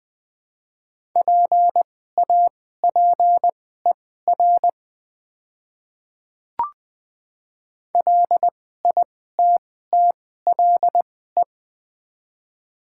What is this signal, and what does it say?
Morse code, heard as a keyed tone, about 20 wpm, spaced as for 14 wpm, 690 Hz. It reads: PAPER LITTLE
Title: 2x Character Spacing - Top 300 Words - 20wpm